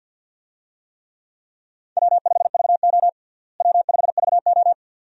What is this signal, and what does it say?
Morse code, heard as a keyed tone, about 52 wpm, 700 Hz. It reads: W5VO W5VO
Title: USA Call Signs - 4 Characters - Repeated 2x - 50wpm